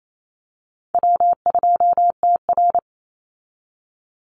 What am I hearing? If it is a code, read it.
Morse code, heard as a keyed tone, about 28 wpm, 700 Hz. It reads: W2TR
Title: Easy Call Signs Repeated 2x - 28wpm